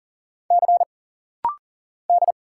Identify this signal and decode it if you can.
Morse code, heard as a keyed tone, about 39 wpm, 700 Hz. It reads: C D
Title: Single Letter-Number - Rapid-Fire 40wpm